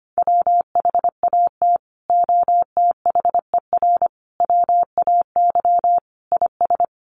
Morse code, heard as a keyed tone, about 25 wpm, 700 Hz. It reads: WHAT OTHER WAY IS
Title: Sentences from Top 100 Words 25wpm